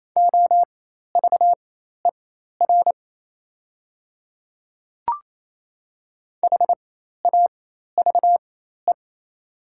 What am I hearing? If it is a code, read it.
Morse code, heard as a keyed tone, about 28 wpm, spaced as for 13 wpm, 700 Hz. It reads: OVER HAVE
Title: 4x Character Spacing - Top 100 Words - 28wpm